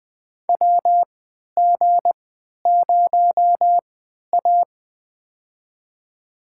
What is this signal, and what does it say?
Morse code, heard as a keyed tone, about 21 wpm, spaced as for 12 wpm, 700 Hz. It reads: WG0A
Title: USA Call Signs - 4 Characters - 3x Character Spacing - 20wpm